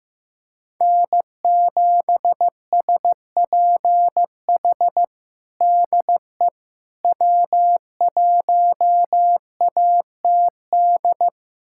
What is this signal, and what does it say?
Morse code, heard as a keyed tone, about 15 wpm, 700 Hz. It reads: N7SPH DE W1ATD
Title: USA Callsign DE Callsign - 5 Characters - 15wpm